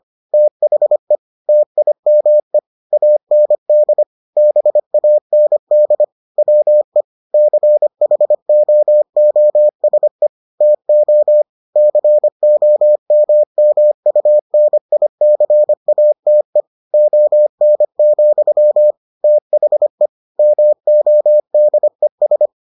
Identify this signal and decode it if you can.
Morse code, heard as a keyed tone, about 25 wpm, 600 Hz. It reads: THE TIME AND BAND WE CHOOSE TO COMMUNICATE ON, THE MODES